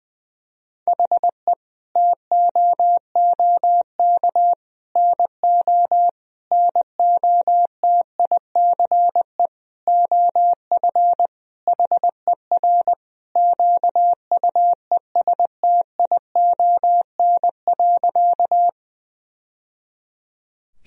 Morse code, heard as a keyed tone, about 20 wpm, 700 Hz. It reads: HE TOOK NO NOTICE OF HER QUESTION.
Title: Sentences from Top 1000 Words - Review Part 2 - 20wpm